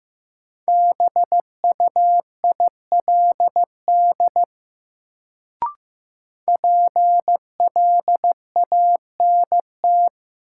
Morse code, heard as a keyed tone, about 15 wpm, 700 Hz. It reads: BUILD PLANT
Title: Sentences from Top 1000 Words - Review Part 1 - New Word Drill - 15wpm